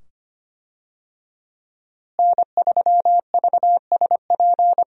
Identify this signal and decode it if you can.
Morse code, heard as a keyed tone, about 25 wpm, 700 Hz. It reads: N3VSP
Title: USA Call Signs - 5 Characters - Repeated 2x - 25wpm